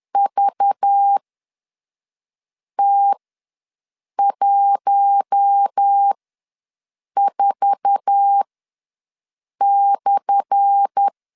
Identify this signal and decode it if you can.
Morse code, heard as a keyed tone, about 10 wpm, spaced as for 5 wpm, 800 Hz. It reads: VT14/
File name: LEZIONI / 10 SIGNS / 10D COMA ws 10wpm 4314